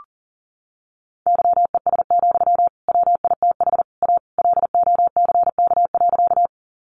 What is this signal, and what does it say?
Morse code, heard as a keyed tone, about 40 wpm, 700 Hz. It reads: YES, WITH A LOCK.
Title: Sentences from Top 1300 Words 40wpm